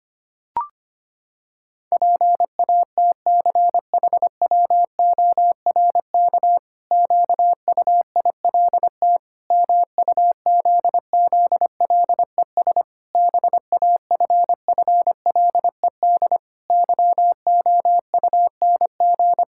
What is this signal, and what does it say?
Morse code, heard as a keyed tone, about 25 wpm, 700 Hz. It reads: PATCHWORK QUILT MUZZLES BAFFLED YOUNG
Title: Pangram Challenge v2 - 25wpm